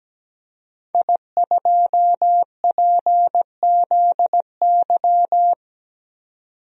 Morse code, heard as a keyed tone, about 17 wpm, 700 Hz. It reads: I2PZY
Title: DXCC-100 Call Signs - 5 Characters - 17wpm